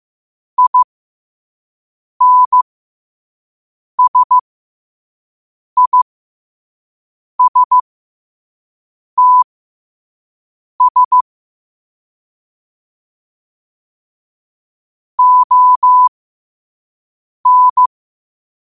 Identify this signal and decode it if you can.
Morse code, heard as a keyed tone, about 15 wpm, spaced as for 5 wpm, 1000 Hz. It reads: INSISTS ON
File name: qotdfiles / 5wpm